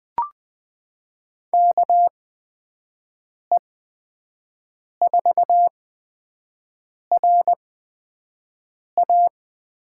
Morse code, heard as a keyed tone, about 21 wpm, spaced as for 5 wpm, 700 Hz. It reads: KE4RA
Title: USA Call Signs - 8x Character Spacing - 20wpm